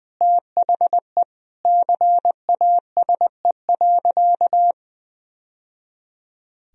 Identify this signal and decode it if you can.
Morse code, heard as a keyed tone, about 21 wpm, 700 Hz. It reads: THE CASE.
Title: Sentences from Top 1000 Words - Review Part 3 - 20wpm